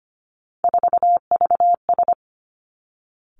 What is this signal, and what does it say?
Morse code, heard as a keyed tone, about 25 wpm, 700 Hz. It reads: 4VS
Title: Three Challenging Letter-Number Combinations 25wpm